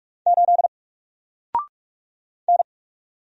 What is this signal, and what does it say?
Morse code, heard as a keyed tone, about 44 wpm, 700 Hz. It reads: 8 N
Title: Single Character (Letter:Number:Punctuation:Prosign) 45wpm